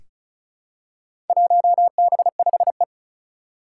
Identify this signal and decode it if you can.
Morse code, heard as a keyed tone, about 36 wpm, 700 Hz. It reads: 1B5E
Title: Four Challenging Letter-Number Combinations 35wpm